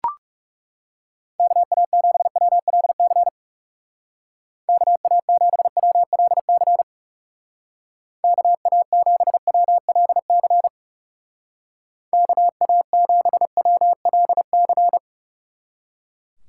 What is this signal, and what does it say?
Morse code, heard as a keyed tone, about 37 wpm, 700 Hz. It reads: KA7WLC KA7WLC KA7WLC KA7WLC
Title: Speed Racing - USA 2x3 Call Signs 30wpm